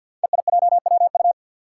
Morse code, heard as a keyed tone, about 53 wpm, 700 Hz. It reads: EIJWU
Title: Five Challenging Letter-Number Combinations 50wpm